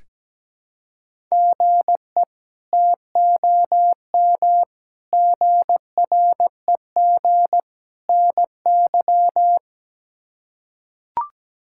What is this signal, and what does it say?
Morse code, heard as a keyed tone, about 17 wpm, 700 Hz. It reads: GE TOM GREG NY
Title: K1USN SST QSO Practice - 17wpm